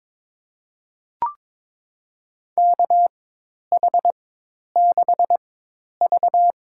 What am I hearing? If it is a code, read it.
Morse code, heard as a keyed tone, about 22 wpm, spaced as for 10 wpm, 700 Hz. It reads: KH6V
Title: USA Call Signs - 4 Characters - 4x Character Spacing - 22wpm